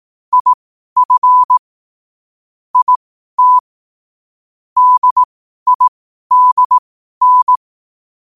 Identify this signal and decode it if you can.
Morse code, heard as a keyed tone, about 18 wpm, spaced as for 12 wpm, 1000 Hz. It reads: IF IT DIDN
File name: qotdfiles / 12wpm